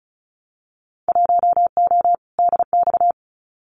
Morse code, heard as a keyed tone, about 36 wpm, 700 Hz. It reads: 1O DX
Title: Winter Field Day Exchanges Encore 35wpm